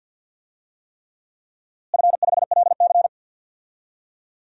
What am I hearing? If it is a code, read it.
Morse code, heard as a keyed tone, about 51 wpm, 700 Hz. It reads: F5LC
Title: DXCC-100 Call Signs - 4 Characters - Repeated 3x - 50wpm